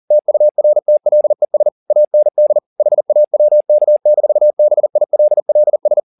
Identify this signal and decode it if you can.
Morse code, heard as a keyed tone, about 40 wpm, 600 Hz. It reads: TURTLES AND HAWK-BILLS